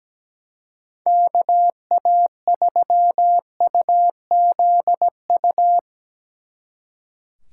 Morse code, heard as a keyed tone, about 19 wpm, 700 Hz. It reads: KA3UZU
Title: Speed Racing - USA 2x3 Call Signs 17wpm